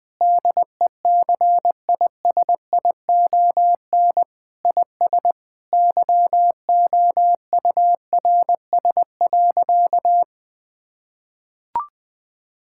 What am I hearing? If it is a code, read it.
Morse code, heard as a keyed tone, about 20 wpm, 700 Hz. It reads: DECISION IS YOURS.
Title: Sentences with 8-Word Phrases - 20wpm